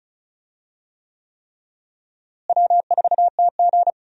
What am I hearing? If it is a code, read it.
Morse code, heard as a keyed tone, about 36 wpm, 700 Hz. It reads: W4TG
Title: USA Call Signs - 4 Characters - Repeated 3x - 35wpm